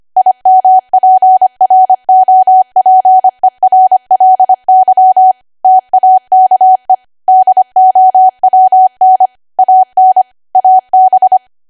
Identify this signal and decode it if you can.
Morse code, heard as a keyed tone, about 25 wpm, 750 Hz. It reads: IMPROPERLY TAKE DOWN AN AB
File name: Archive / 25 WPM / 25WPM